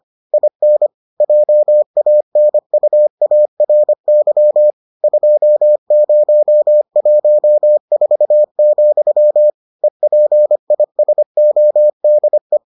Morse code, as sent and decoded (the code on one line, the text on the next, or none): .. -. / .--- .- -. ..- .- .-. -.-- / ..--- ----- .---- ....- --..-- / . .--. .. ... --- -.. .
IN JANUARY 2014, EPISODE